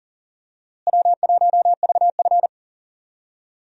Morse code, heard as a keyed tone, about 40 wpm, 700 Hz. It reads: W1VF
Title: USA Call Signs - 4 Characters - Repeated 3x - 40wpm